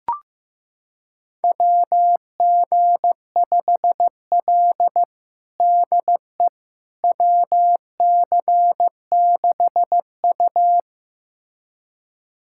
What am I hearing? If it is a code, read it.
Morse code, heard as a keyed tone, about 16 wpm, 700 Hz. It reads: WG5L DE WC6U
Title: USA Callsign DE Callsign - 4 Characters - 15wpm